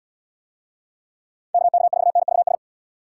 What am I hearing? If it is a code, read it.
Morse code, heard as a keyed tone, about 74 wpm, 700 Hz. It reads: KK4NVS